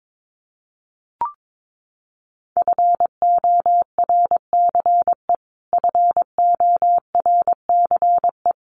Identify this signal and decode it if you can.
Morse code, heard as a keyed tone, about 22 wpm, 700 Hz. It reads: FORCE FORCE
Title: Top 500 Words Repeated 22wpm